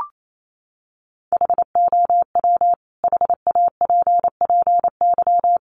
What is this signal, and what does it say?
Morse code, heard as a keyed tone, about 28 wpm, 700 Hz. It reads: HOW HAPPY
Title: Sentences from Top 1000 Words 28wpm